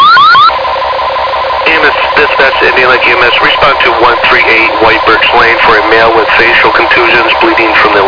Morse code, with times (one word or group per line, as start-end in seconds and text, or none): none